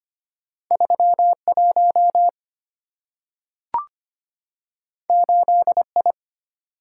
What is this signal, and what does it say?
Morse code, heard as a keyed tone, about 25 wpm, 700 Hz. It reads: 31 8I
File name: Two Letter-Number Combinations - 25wpm